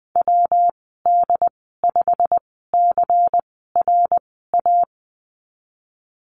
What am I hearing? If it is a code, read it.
Morse code, heard as a keyed tone, about 20 wpm, spaced as for 15 wpm, 700 Hz. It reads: WD5CRA
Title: 128 Lesson 21g - Number 6 - Call Signs 20wpm